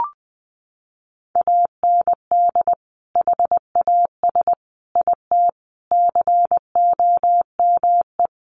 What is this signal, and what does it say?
Morse code, heard as a keyed tone, about 20 wpm, 700 Hz. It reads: AND HAS IT COME
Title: Sentences from Top 100 Words 20wpm